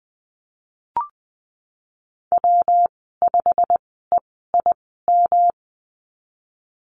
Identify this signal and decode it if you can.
Morse code, heard as a keyed tone, about 20 wpm, spaced as for 15 wpm, 700 Hz. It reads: W5EIM
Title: 161 Lesson 26g - Letter g - Call Signs 20wpm